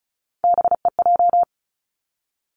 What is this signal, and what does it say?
Morse code, heard as a keyed tone, about 36 wpm, 700 Hz. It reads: BEJ
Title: Three Challenging Letter-Number Combinations 35wpm